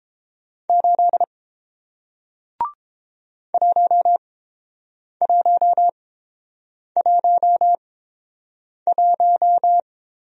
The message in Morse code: 8 1111

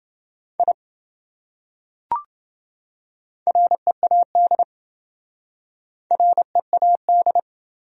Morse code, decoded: I READ READ

